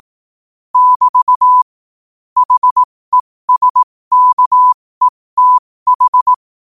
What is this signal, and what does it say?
Morse code, heard as a keyed tone, about 18 wpm, spaced as for 15 wpm, 1000 Hz. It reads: <BT> HESKETH
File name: qotdfiles / 15wpm